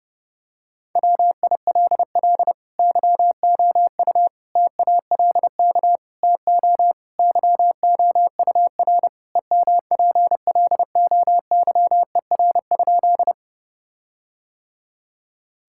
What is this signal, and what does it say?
Morse code, heard as a keyed tone, about 30 wpm, 700 Hz. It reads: WILL YOU TALK TO YOUR EMPLOYER?
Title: Sentences from Top 1600 Words 30wpm